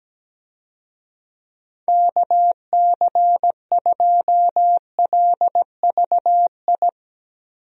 Morse code, heard as a keyed tone, about 17 wpm, 700 Hz. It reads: KC2LVI
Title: USA Call Signs - 6 Characters - 17wpm